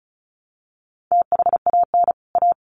Morse code, heard as a keyed tone, about 34 wpm, 700 Hz. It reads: THAN A
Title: Sets of 2 Words 35wpm